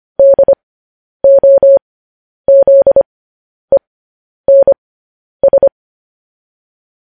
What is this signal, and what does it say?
Morse code, heard as a keyed tone, about 25 wpm, spaced as for 10 wpm, 550 Hz. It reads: DOZENS
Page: news-headlines-in-morse-code-at-10-wpm